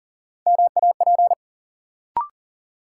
Morse code, heard as a keyed tone, about 40 wpm, 700 Hz. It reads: MAP
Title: Sentences from Top 400 Words - New Word Drill - 40wpm